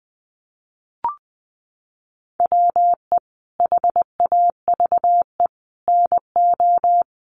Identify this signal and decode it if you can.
Morse code, heard as a keyed tone, about 20 wpm, 700 Hz. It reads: WE HAVE NO
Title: Sentences from Top 100 Words 20wpm